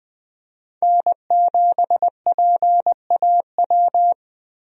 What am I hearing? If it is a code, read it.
Morse code, heard as a keyed tone, about 20 wpm, 700 Hz. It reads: N7PAW
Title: USA Call Signs - 5 Characters - Repeated 3x - 20wpm